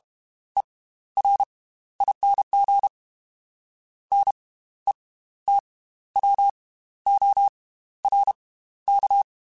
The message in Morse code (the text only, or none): ERING NETWORK